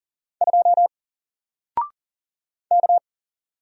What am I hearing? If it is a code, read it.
Morse code, heard as a keyed tone, about 40 wpm, 700 Hz. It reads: 2 K
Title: Single Letter-Number 40wpm